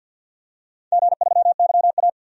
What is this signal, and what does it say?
Morse code, heard as a keyed tone, about 48 wpm, 700 Hz. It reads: G3YA
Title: DXCC-100 Call Signs - 4 Characters - 50wpm